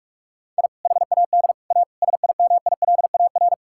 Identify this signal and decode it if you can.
Morse code, heard as a keyed tone, about 46 wpm, 700 Hz. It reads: I HAD A SIMILAR
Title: Sentences from Top 1000 Words - Review Part 5 - 45wpm